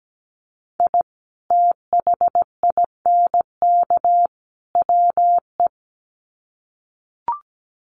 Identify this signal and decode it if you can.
Morse code, heard as a keyed tone, about 17 wpm, 700 Hz. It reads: I THINK WE
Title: Sets of 3 Words 17wpm